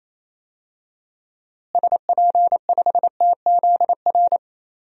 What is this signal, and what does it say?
Morse code, heard as a keyed tone, about 27 wpm, 700 Hz. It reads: SP5TZR
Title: DXCC-100 Call Signs - 6 Characters - Repeated 3x - 28wpm